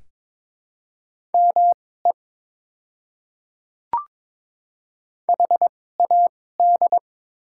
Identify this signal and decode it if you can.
Morse code, heard as a keyed tone, about 22 wpm, spaced as for 16 wpm, 700 Hz. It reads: ME HAD
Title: 2x Character Spacing - Top 200 Words - 22wpm